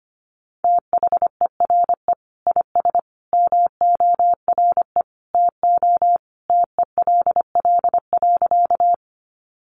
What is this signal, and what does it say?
Morse code, heard as a keyed tone, about 25 wpm, 700 Hz. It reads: THERE IS MORE TO TELL.
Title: Sentences from Top 200 Words 25wpm